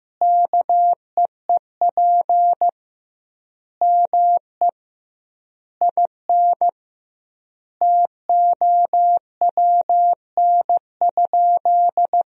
Morse code, heard as a keyed tone, about 15 wpm, 700 Hz. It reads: KEEP ME IN TOWN?
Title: 2x Word-Spacing - Sentences from Top 500 Words - 15wpm